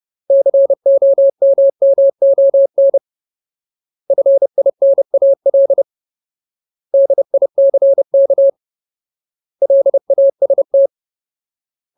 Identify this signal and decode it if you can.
Morse code, heard as a keyed tone, about 30 wpm, 550 Hz. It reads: COMMON FINAL DICK LAST
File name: Ultimate Word List / 30WPM Ultimate